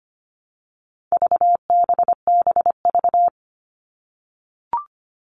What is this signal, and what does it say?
Morse code, heard as a keyed tone, about 25 wpm, 700 Hz. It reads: VBBV
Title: All Cut Numbers - 4-Digit 25wpm